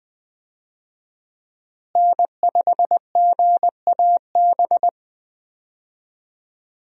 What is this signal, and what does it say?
Morse code, heard as a keyed tone, about 20 wpm, 700 Hz. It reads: N5GAB
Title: USA Call Signs - 5 Characters - Repeated 2x - 20wpm